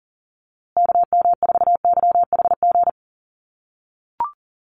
Morse code, heard as a keyed tone, about 40 wpm, 700 Hz. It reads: KM4YHG
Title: USA Call Signs Repeated 2x 40wpm